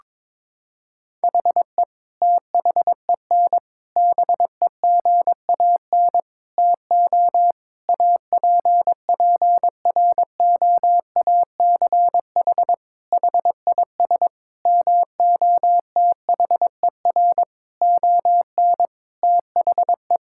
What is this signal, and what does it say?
Morse code, heard as a keyed tone, about 22 wpm, 700 Hz. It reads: HE THEN BEGAN TO APPROACH HIS MOTHER ON THE
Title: Sentences from Top 1000 Words - Review Part 6 - 22wpm